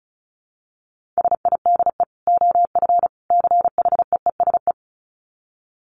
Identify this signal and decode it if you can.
Morse code, heard as a keyed tone, about 34 wpm, 700 Hz. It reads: SIDE OF CHEESE
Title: Dit Challenge 35wpm